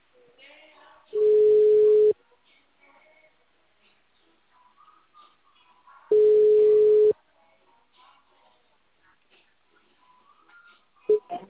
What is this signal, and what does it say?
Ringback tone (Europe, 425 Hz).
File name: out- -2001- - - .35